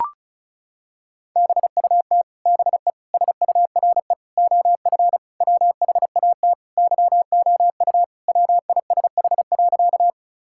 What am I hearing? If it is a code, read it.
Morse code, heard as a keyed tone, about 34 wpm, 700 Hz. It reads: BUT BE SURE OF WHAT YOU WISH.
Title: Sentences from Top 1600 Words 35wpm